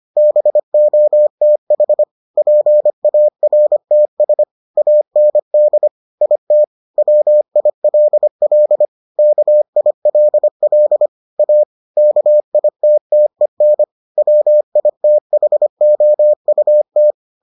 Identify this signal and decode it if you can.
Morse code, heard as a keyed tone, about 25 wpm, 600 Hz. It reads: BOTH PARTS AND IT WILL KILL A KITTEN WITHOUT